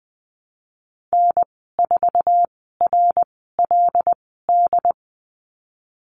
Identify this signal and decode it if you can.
Morse code, heard as a keyed tone, about 20 wpm, spaced as for 15 wpm, 700 Hz. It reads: N4RLD
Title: 154 Lesson 25g - Letter p - Call Signs 20wpm